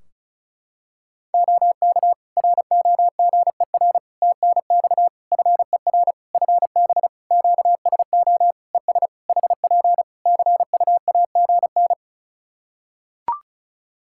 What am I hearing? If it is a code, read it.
Morse code, heard as a keyed tone, about 36 wpm, 700 Hz. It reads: OK ROGER TNX FER FB QSO ES HP CUAGN
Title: Practice QSO - Exchange 4 - Ending - 35wpm